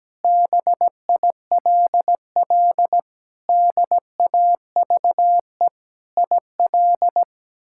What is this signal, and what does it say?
Morse code, heard as a keyed tone, about 17 wpm, 700 Hz. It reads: BILL DAVE IL
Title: K1USN SST QSO Practice - 17wpm